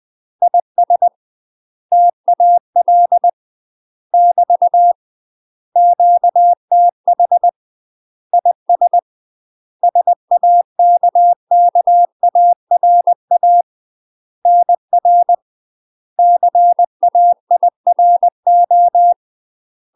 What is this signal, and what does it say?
Morse code, heard as a keyed tone, about 20 wpm, 700 Hz. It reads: IS TAL <BT> QTH IS SAKKARA NR CAIRO
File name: UK MASTER QSOs TRAINING / 20WPM / UK MASTER QSOs 0005